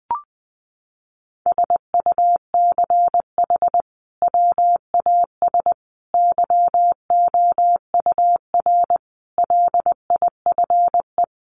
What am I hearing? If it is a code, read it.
Morse code, heard as a keyed tone, about 19 wpm, 700 Hz. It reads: SUCH WAS YOUR LIFE
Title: Sentences from Top 1500 Words 20wpm